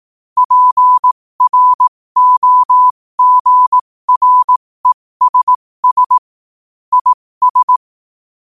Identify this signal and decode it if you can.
Morse code, heard as a keyed tone, about 18 wpm, spaced as for 15 wpm, 1000 Hz. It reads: PROGRESS IS
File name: qotdfiles / 15wpm